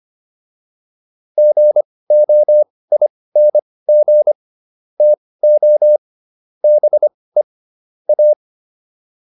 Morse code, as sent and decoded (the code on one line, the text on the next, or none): --. --- .. -. --. / - --- / -... / .-
GOING TO BE A